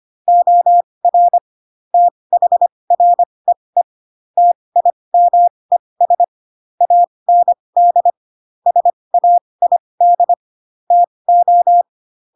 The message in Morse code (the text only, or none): OR THREE TIMES AND SAID TO